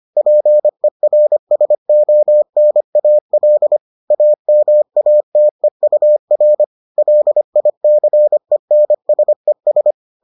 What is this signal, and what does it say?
Morse code, heard as a keyed tone, about 25 wpm, 600 Hz. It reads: PERSONAL AMATEUR LICENSES